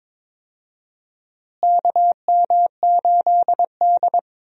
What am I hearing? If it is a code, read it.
Morse code, heard as a keyed tone, about 22 wpm, 700 Hz. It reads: KM8D
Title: USA Call Signs - 4 Characters - Repeated 2x - 22wpm